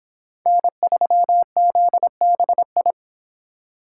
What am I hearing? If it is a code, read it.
Morse code, heard as a keyed tone, about 23 wpm, 700 Hz. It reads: N3ZBI